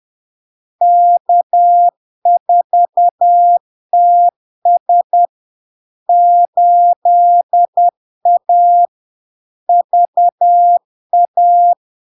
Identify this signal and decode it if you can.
Morse code, heard as a keyed tone, about 10 wpm, 700 Hz. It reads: K4TS 8A VA
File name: LARK-FD-2022-Log-in-CW-10wpm-700-Hz